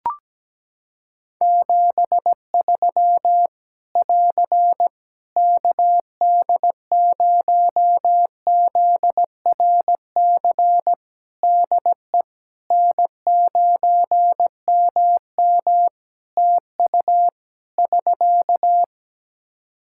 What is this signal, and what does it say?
Morse code, heard as a keyed tone, about 17 wpm, 700 Hz. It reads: 73 <AR> KD0ZRC DE N9MM TU <SK>
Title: Practice QSO - Exchange 4 - Ending - 17wpm